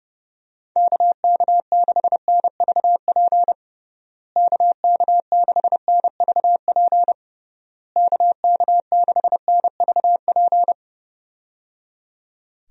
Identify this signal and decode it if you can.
Morse code, heard as a keyed tone, about 30 wpm, 700 Hz. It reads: KK6NVP KK6NVP KK6NVP
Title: USA Call Signs - 6 Characters - Repeated 3x - 30wpm